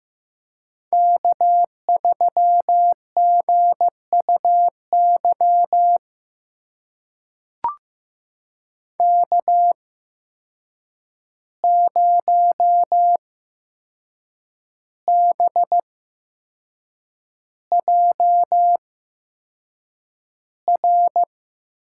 Morse code, heard as a keyed tone, about 15 wpm, spaced as for 6 wpm, 700 Hz. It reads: K3GUY K0BJR
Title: USA Call Signs - 5 Characters - 8x Character Spacing - 15wpm